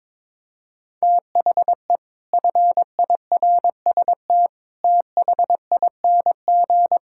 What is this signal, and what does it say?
Morse code, heard as a keyed tone, about 22 wpm, 700 Hz. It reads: THE FIRST THING
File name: practice / 22wpm / Sentences with 8-Word Phrases - 22wpm